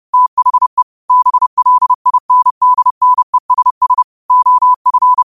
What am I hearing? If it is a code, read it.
Morse code, heard as a keyed tone, about 30 wpm, 1000 Hz. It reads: THE BLINDNESS OF